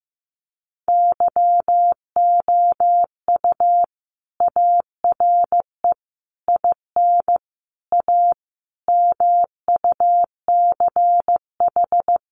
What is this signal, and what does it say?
Morse code, heard as a keyed tone, about 15 wpm, 700 Hz. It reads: YOU ARE IN A MUCH
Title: Sentences from Top 1400 Words 15wpm